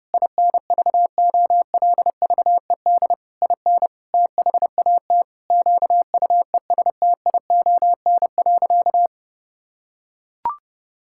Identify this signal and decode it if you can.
Morse code, heard as a keyed tone, about 30 wpm, 700 Hz. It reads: INVOLVED IN THAT QUESTION.
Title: Sentences from Top 1000 Words - Review Part 5 - 30wpm